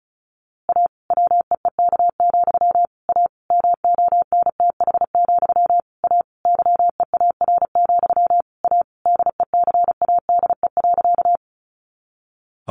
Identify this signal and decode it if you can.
Morse code, heard as a keyed tone, about 34 wpm, 700 Hz. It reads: A WEEK, A MONTH, A YEAR, A DECADE.
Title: A Sound Of Thunder 35wpm